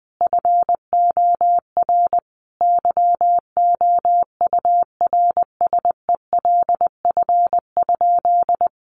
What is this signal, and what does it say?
Morse code, heard as a keyed tone, about 20 wpm, 700 Hz. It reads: FOR YOURSELF?
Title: Sentences from Top 1500 Words 20wpm